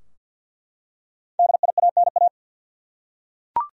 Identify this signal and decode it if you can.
Morse code, heard as a keyed tone, about 47 wpm, 700 Hz. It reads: DIANA
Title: USA Common Names - Max 5 Letters - 50wpm